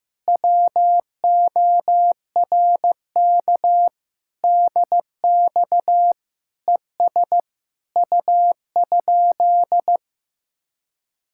Morse code, heard as a keyed tone, about 15 wpm, 700 Hz. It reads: WORK DX ES U?
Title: Speed-Racing Common QSO Phrases Encore - 15wpm